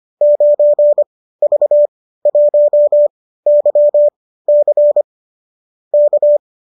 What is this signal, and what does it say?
Morse code, heard as a keyed tone, about 25 wpm, spaced as for 15 wpm, 590 Hz. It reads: 9V1YC K